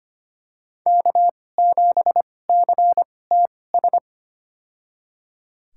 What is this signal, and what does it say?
Morse code, heard as a keyed tone, about 25 wpm, spaced as for 18 wpm, 700 Hz. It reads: K7CTS
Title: USA Call Signs - 5 Characters - 2x Character Spacing - 25wpm